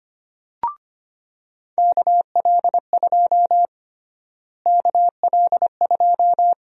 Morse code, heard as a keyed tone, about 25 wpm, 700 Hz. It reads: KL2 KL2
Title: USA Call Sign Prefixes - Repeated 2x - 25wpm